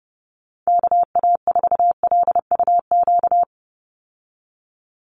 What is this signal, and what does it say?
Morse code, heard as a keyed tone, about 30 wpm, 700 Hz. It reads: KA4LUQ
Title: USA Call Signs 30wpm